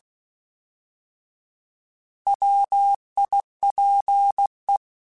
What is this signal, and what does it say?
Morse code, heard as a keyed tone, about 16 wpm, 800 Hz. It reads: WIPE
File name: Alphabetical / Alphabetical mp3s / Ws @15